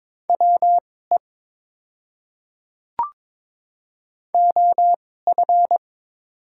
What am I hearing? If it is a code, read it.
Morse code, heard as a keyed tone, about 22 wpm, spaced as for 16 wpm, 700 Hz. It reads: WE OF
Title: 2x Character Spacing - Top 100 Words - 22wpm